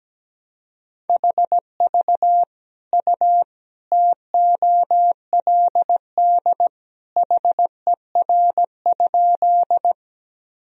Morse code, heard as a keyed tone, about 17 wpm, 700 Hz. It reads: HV U TOLD HER?
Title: Sentences from Top 400 Words and Abbreviations 17wpm